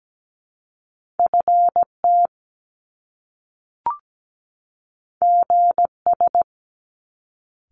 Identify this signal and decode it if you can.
Morse code, heard as a keyed tone, about 17 wpm, 700 Hz. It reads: FT GS
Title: Top Two Letter Combinations in Words 17wpm